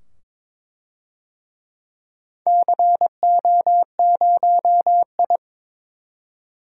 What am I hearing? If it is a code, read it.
Morse code, heard as a keyed tone, about 22 wpm, 700 Hz. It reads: CO0I
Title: DXCC-100 Call Signs - 4 Characters - 22wpm